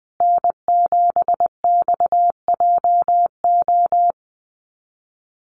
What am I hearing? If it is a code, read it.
Morse code, heard as a keyed tone, about 20 wpm, 700 Hz. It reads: N7XJO
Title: USA Call Signs Repeated 2x 20wpm